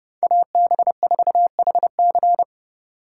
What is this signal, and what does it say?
Morse code, heard as a keyed tone, about 36 wpm, 700 Hz. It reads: AB4HC
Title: Speed Racing - USA 2x2 Call Signs 30wpm